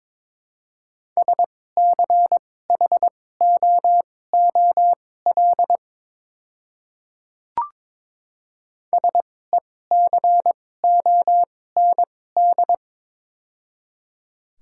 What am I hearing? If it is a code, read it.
Morse code, heard as a keyed tone, about 22 wpm, spaced as for 16 wpm, 700 Hz. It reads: SCHOOL SECOND
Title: 2x Character Spacing - Top 300 Words - 22wpm